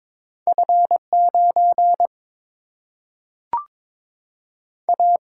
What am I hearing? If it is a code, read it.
Morse code, heard as a keyed tone, about 22 wpm, 700 Hz. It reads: F 9 A
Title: Single-Letter Single-Number 22wpm